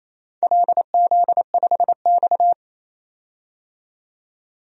Morse code, heard as a keyed tone, about 28 wpm, 700 Hz. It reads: LZ5X
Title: DXCC-100 Call Signs - 4 Characters - 28wpm